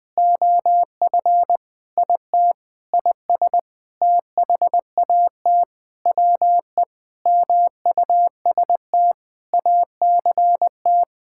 Morse code, heard as a keyed tone, about 20 wpm, 700 Hz. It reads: OF IT IS THAT WE MUST ACT
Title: Sentences with 9-Word Phrases - Spoken Beginning - 20wpm